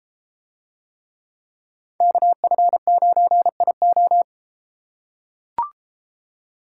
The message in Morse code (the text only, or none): KF9IO